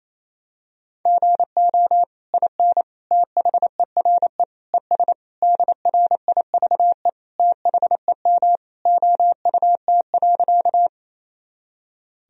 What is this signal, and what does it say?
Morse code, heard as a keyed tone, about 28 wpm, 700 Hz. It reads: GO IN THERE ES DRIVE THEM OUT.
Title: Sentences from Top 500 Words and Abbreviations 28wpm